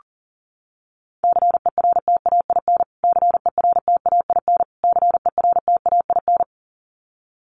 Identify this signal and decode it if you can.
Morse code, heard as a keyed tone, about 40 wpm, 700 Hz. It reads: CERTAIN CERTAIN CERTAIN
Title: Top 500 Words Repeated 40wpm